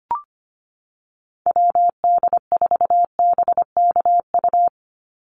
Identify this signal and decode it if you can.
Morse code, heard as a keyed tone, about 25 wpm, 700 Hz. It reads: WD4BKU
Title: USA Call Signs 25wpm